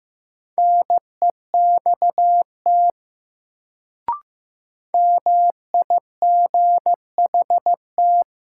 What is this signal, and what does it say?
Morse code, heard as a keyed tone, about 15 wpm, 700 Hz. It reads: NEXT MIGHT
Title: Sentences from Top 300 Words - New Word Drill - 15wpm